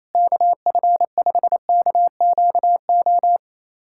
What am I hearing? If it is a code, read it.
Morse code, heard as a keyed tone, about 28 wpm, 700 Hz. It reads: KF5KQO
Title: USA Callsign DE Callsign - 6 Characters - 28wpm